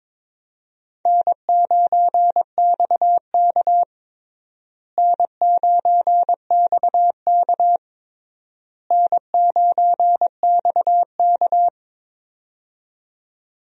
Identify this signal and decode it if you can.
Morse code, heard as a keyed tone, about 22 wpm, 700 Hz. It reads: N9XK N9XK N9XK
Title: USA Call Signs - 4 Characters - Repeated 3x - 22wpm